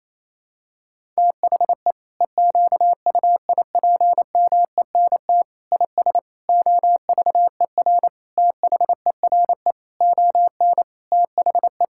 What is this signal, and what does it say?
Morse code, heard as a keyed tone, about 28 wpm, 700 Hz. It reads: THE EQUIPMENT IS OVER THERE ON THE
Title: Sentences from Top 1000 Words - Review Part 5 - 28wpm